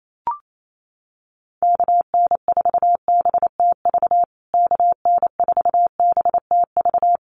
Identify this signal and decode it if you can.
Morse code, heard as a keyed tone, about 28 wpm, 700 Hz. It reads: KN4BTV KN4BTV
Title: USA Call Signs Repeated 2x 28wpm